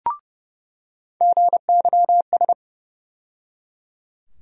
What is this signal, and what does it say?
Morse code, heard as a keyed tone, about 30 wpm, 700 Hz. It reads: GYS
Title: USA Call Sign Postfixes - 3 Letters 30wpm